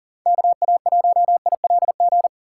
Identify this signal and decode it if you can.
Morse code, heard as a keyed tone, about 52 wpm, 700 Hz. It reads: KA1ILG